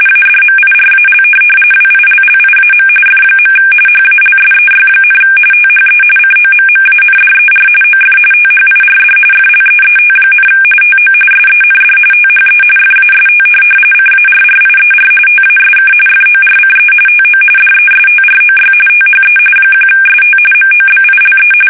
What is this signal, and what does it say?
RTTY, about 45.45 baud. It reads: ADMIRAL WILLIAM F. HALSEY, THE COMMANDER OF THIRD FLEET, TO SEND THE FAST CARRIER TASK FORCE TO DESTROY THE 1ST MOBILE
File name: fsk850 / USS-Massachusetts